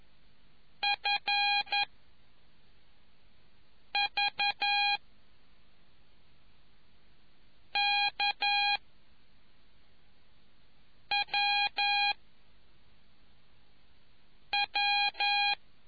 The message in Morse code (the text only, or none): F V K W W